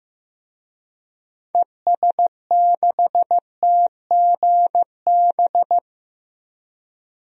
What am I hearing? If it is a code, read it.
Morse code, heard as a keyed tone, about 15 wpm, 700 Hz. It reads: ES6TGB
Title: DXCC-100 Call Signs - 6 Characters - 15wpm